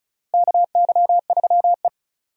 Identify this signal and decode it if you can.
Morse code, heard as a keyed tone, about 36 wpm, 700 Hz. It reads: KY3E
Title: USA Call Signs - 4 Characters - Repeated 2x - 35wpm